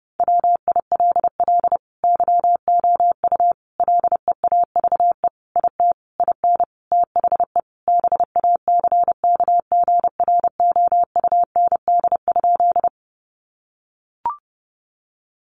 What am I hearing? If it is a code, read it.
Morse code, heard as a keyed tone, about 30 wpm, 700 Hz. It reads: WILL YOU LEAVE IT IN THE BACKGROUND?
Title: Sentences from Top 1300 Words 30wpm